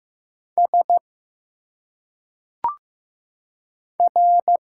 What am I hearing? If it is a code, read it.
Morse code, heard as a keyed tone, about 15 wpm, 700 Hz. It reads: S R
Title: Single Letters - Phonetic Pronunciation 15wpm